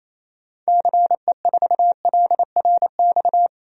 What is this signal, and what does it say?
Morse code, heard as a keyed tone, about 28 wpm, 700 Hz. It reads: CE4LRX
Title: DXCC-100 Call Signs - 6 Characters - 28wpm